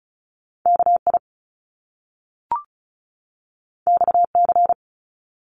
Morse code, heard as a keyed tone, about 36 wpm, 700 Hz. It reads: KI XC
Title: Top Two Letter Combinations in Words 35wpm